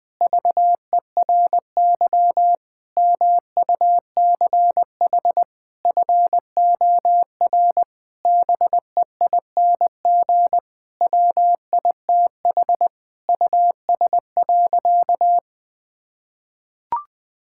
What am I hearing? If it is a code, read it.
Morse code, heard as a keyed tone, about 21 wpm, 700 Hz. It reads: VERY MUCH FOR BEING WITH US.
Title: Sentences with 5-Word Phrases - 20wpm